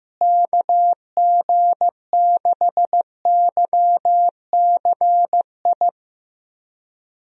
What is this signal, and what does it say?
Morse code, heard as a keyed tone, about 15 wpm, 700 Hz. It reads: KG6YCI
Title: USA Call Signs - 6 Characters - Repeated 3x - 15wpm